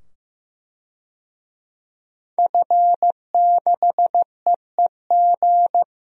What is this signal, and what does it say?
Morse code, heard as a keyed tone, about 15 wpm, 700 Hz. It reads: F6EEG
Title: DXCC-100 Call Signs - 5 Characters - 15wpm